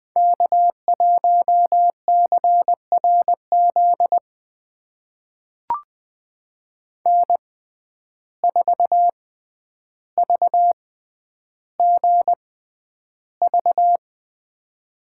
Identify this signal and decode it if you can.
Morse code, heard as a keyed tone, about 20 wpm, spaced as for 10 wpm, 700 Hz. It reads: K1CRZ N4VGV
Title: USA Call Signs - 5 Characters - 6x Character Spacing - 20wpm